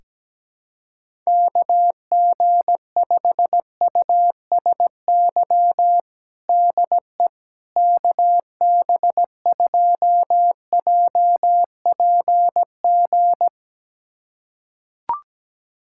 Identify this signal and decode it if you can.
Morse code, heard as a keyed tone, about 17 wpm, 700 Hz. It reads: KG5USY DE KB2JPG
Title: USA Callsign DE Callsign - 6 Characters - 17wpm